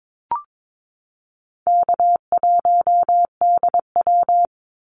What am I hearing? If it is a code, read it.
Morse code, heard as a keyed tone, about 22 wpm, 700 Hz. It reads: K1DW
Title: Easy Call Signs - 22wpm